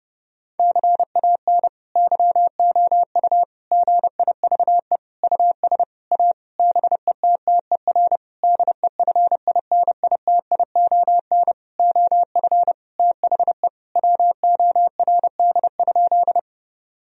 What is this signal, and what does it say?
Morse code, heard as a keyed tone, about 32 wpm, 700 Hz. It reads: CAN YOU GIVE US A BETTER DEFINITION OF THE WORD?
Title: Sentences from Top 1000 Words - Review Part 4 - 30wpm